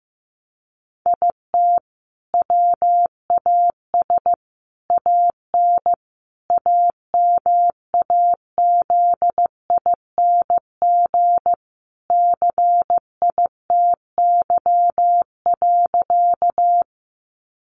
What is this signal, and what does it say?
Morse code, heard as a keyed tone, about 15 wpm, 700 Hz. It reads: IT WAS AN AMAZING CITY.
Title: Sentences from Top 1500 Words 15wpm